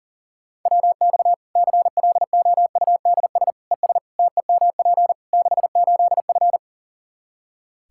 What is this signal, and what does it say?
Morse code, heard as a keyed tone, about 40 wpm, 700 Hz. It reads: WX CLOUDS ES TEMP 68F
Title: Practice QSO - Exchange 2 - Info - 40wpm